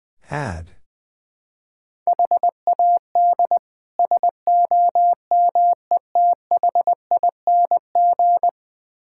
Morse code, heard as a keyed tone, about 20 wpm, 700 Hz. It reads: HAD SOMETHING
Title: Build A Phrase - 5-Word Phrases - 20wpm